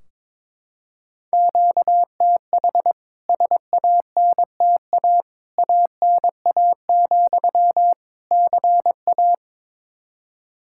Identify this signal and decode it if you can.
Morse code, heard as a keyed tone, about 21 wpm, 700 Hz. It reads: QTH SANTA ANA, CA
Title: K1USN SST QSO Practice 22wpm